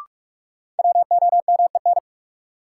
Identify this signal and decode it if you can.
Morse code, heard as a keyed tone, about 46 wpm, 700 Hz. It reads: WOMEN
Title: Sentences from Top 1300 Words - New Word Drill - 45wpm